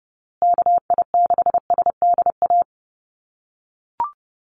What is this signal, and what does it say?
Morse code, heard as a keyed tone, about 30 wpm, 700 Hz. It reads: KI6SDA
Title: USA Call Signs Repeated 3x 30wpm